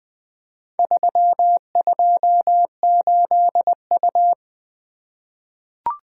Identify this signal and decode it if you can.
Morse code, heard as a keyed tone, about 20 wpm, 700 Hz. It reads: 328U
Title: Four Letter-Number Combinations - 20wpm